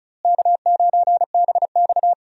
Morse code, heard as a keyed tone, about 36 wpm, 700 Hz. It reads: K9BX
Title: USA Call Signs - 4 Characters - Repeated 2x - 35wpm